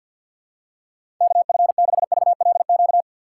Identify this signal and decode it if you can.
Morse code, heard as a keyed tone, about 53 wpm, 700 Hz. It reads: K F 6 V L Q
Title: USA Call Signs - 6 Characters - 8x Character Spacing - 50wpm